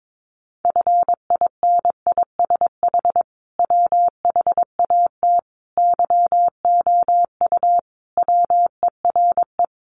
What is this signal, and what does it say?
Morse code, heard as a keyed tone, about 22 wpm, 700 Hz. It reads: FINISH WHAT YOU WERE
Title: Sentences from Top 1200 Words 22wpm